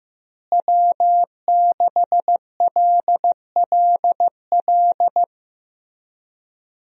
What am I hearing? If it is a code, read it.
Morse code, heard as a keyed tone, about 15 wpm, 700 Hz. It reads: W6LLL
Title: USA Call Signs - 5 Characters - Repeated 3x - 15wpm